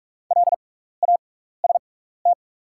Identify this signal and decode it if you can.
Morse code, heard as a keyed tone, about 46 wpm, spaced as for 16 wpm, 690 Hz. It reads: LAST
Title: 6x Character Spacing - Top 500 Words - 45wpm